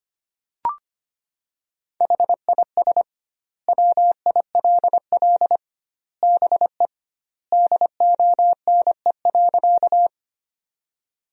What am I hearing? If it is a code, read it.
Morse code, heard as a keyed tone, about 25 wpm, 700 Hz. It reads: HIS WILL BE DONE.
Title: 2x Word-Spacing - Sentences from Top 500 Words - 25wpm